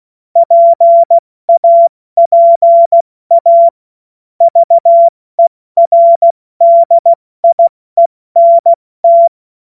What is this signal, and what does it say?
Morse code, heard as a keyed tone, about 16 wpm, 680 Hz. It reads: PAPA VERDIENT
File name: Witz-14-Oktober-2025-70-F80